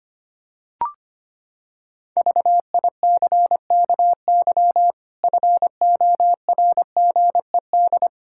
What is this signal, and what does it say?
Morse code, heard as a keyed tone, about 25 wpm, 700 Hz. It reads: VICKY FORGED
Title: Pangram Challenge v2 - 25wpm